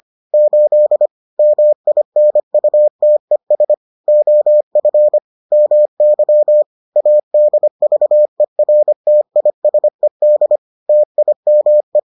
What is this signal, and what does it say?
Morse code, heard as a keyed tone, about 25 wpm, 600 Hz. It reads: 8 MINUTES OF MY ADVERTISED TIME